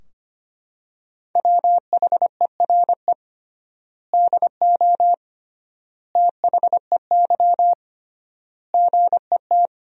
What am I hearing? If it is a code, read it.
Morse code, heard as a keyed tone, about 25 wpm, 700 Hz. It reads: WHERE DO THEY GET
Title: 3x Word-Spacing - Sentences from Top 200 Words - 25wpm